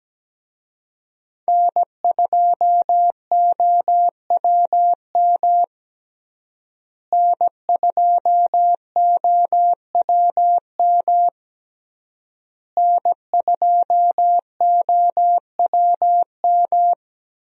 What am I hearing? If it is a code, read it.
Morse code, heard as a keyed tone, about 17 wpm, 700 Hz. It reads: N2OWM N2OWM N2OWM
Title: USA Call Signs - 5 Characters - Repeated 3x - 17wpm